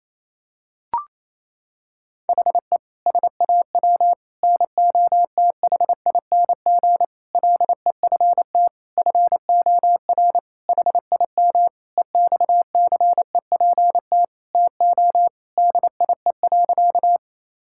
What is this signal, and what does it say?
Morse code, heard as a keyed tone, about 28 wpm, 700 Hz. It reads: HE SAW NOTHING LEFT FOR HIM EXCEPT TO DIE.
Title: Sentences from Top 1600 Words 28wpm